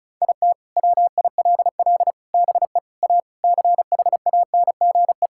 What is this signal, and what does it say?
Morse code, heard as a keyed tone, about 34 wpm, 700 Hz. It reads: IT WILL BE A CHANGE
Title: Sentences from Top 1000 Words - Review Part 6 - 35wpm